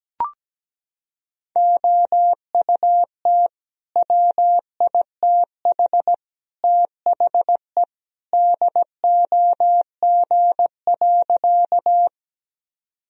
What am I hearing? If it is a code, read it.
Morse code, heard as a keyed tone, about 17 wpm, 700 Hz. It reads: OUT WITH THE DOG.
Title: Sentences from Top 1000 Words - Review Part 2 - 17wpm